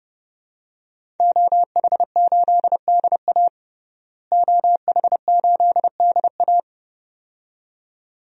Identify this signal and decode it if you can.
Morse code, heard as a keyed tone, about 30 wpm, 700 Hz. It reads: OH8DA OH8DA
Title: DXCC-100 Call Signs - 5 Characters - Repeated 2x - 30wpm